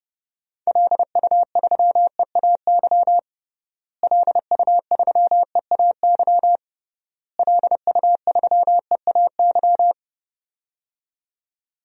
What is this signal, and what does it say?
Morse code, heard as a keyed tone, about 31 wpm, 700 Hz. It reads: LU3EAY LU3EAY LU3EAY
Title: DXCC-100 Call Signs - 6 Characters - Repeated 3x - 30wpm